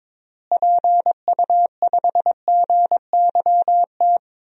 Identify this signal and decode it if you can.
Morse code, heard as a keyed tone, about 22 wpm, 700 Hz. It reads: PU5GYT
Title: DXCC-100 Call Signs - 6 Characters - Repeated 2x - 22wpm